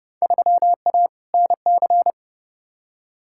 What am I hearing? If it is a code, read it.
Morse code, heard as a keyed tone, about 30 wpm, 700 Hz. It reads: 3A NC
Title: Field Day Exchanges Encore - 30wpm